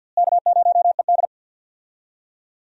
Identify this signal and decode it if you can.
Morse code, heard as a keyed tone, about 47 wpm, 700 Hz. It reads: K0ED
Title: USA Call Signs - 4 Characters - Repeated 3x - 50wpm